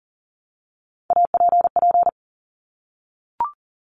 Morse code, heard as a keyed tone, about 40 wpm, 700 Hz. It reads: APP
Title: Top Three Letter Combinations in Words 40wpm